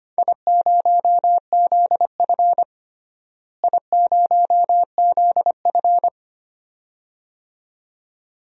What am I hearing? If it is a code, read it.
Morse code, heard as a keyed tone, about 25 wpm, 700 Hz. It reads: I0ZF I0ZF
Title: DXCC-100 Call Signs - 4 Characters - Repeated 2x - 25wpm